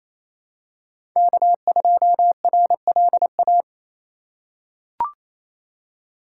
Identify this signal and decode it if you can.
Morse code, heard as a keyed tone, about 27 wpm, 700 Hz. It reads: K2RLA
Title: Code Groups - 5 Letter:Numbers - 28wpm